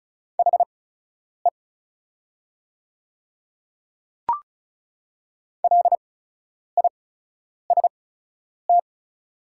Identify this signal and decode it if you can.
Morse code, heard as a keyed tone, about 36 wpm, spaced as for 10 wpm, 690 Hz. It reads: HE LIST
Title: 8x Character Spacing - Top 500 Words - 35wpm